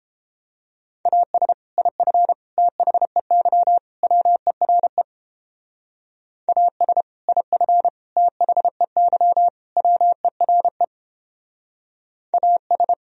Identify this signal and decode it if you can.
Morse code, heard as a keyed tone, about 29 wpm, 700 Hz. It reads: AS IF THEY WERE AS IF THEY WERE AS